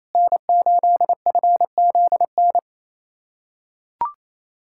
Morse code, heard as a keyed tone, about 28 wpm, 700 Hz. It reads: N8FZN
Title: USA Callsign DE Callsign - 5 Characters - 28wpm